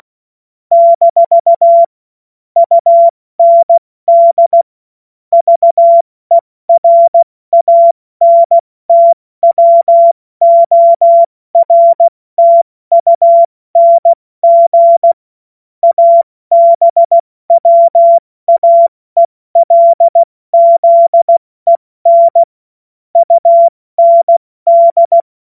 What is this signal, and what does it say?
Morse code, heard as a keyed tone, about 16 wpm, spaced as for 14 wpm, 680 Hz. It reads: - UND VERANTWORTUNG ABWAELZEN UND
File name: Morsekurs / Witze / Witz 24.Oktober 2023 70 F80